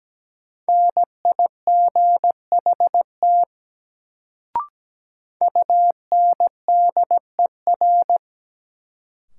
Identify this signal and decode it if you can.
Morse code, heard as a keyed tone, about 17 wpm, 700 Hz. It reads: NIGHT UNDER
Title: Sentences from Top 300 Words - New Word Drill - 17wpm